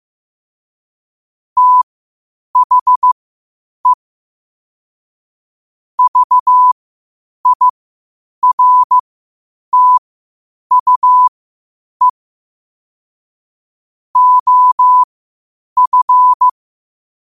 Morse code, heard as a keyed tone, about 15 wpm, spaced as for 8 wpm, 1000 Hz. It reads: THE VIRTUE OF
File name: qotdfiles / 8wpm